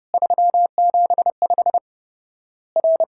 Morse code, heard as a keyed tone, about 30 wpm, 700 Hz. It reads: 375 R
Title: Worked All Europe DX Contest - QTC Exchange - 30wpm